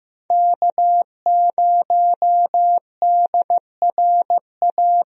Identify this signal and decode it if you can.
Morse code, heard as a keyed tone, about 15 wpm, 700 Hz. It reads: K0DRA
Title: USA Call Signs - 5 Characters - Repeated 2x - 15wpm